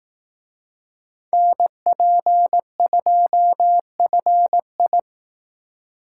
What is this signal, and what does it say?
Morse code, heard as a keyed tone, about 18 wpm, 700 Hz. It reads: NP2FI
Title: Speed Racing - USA 2x2 Call Signs 15wpm